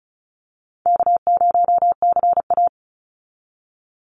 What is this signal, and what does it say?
Morse code, heard as a keyed tone, about 34 wpm, 700 Hz. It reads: K0CA
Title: Call Signs Repeated 2x - 35wpm